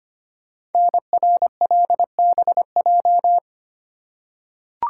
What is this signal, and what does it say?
Morse code, heard as a keyed tone, about 25 wpm, 700 Hz. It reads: NRLBJ
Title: Code Groups - 5 Letters - 25wpm